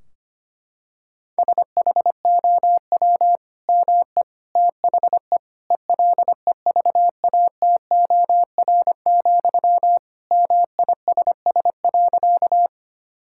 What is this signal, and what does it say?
Morse code, heard as a keyed tone, about 25 wpm, 700 Hz. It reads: SHOW ME THE ELEVATOR, MISS.
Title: Sentences from Top 1600 Words 25wpm